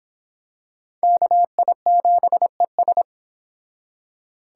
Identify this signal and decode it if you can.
Morse code, heard as a keyed tone, about 25 wpm, 700 Hz. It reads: KI7ES